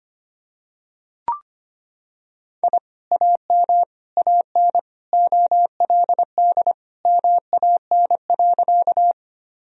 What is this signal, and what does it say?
Morse code, heard as a keyed tone, about 25 wpm, 700 Hz. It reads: I AM AN OLD MAN.
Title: Sentences from Top 1000 Words - Review Part 2 - 25wpm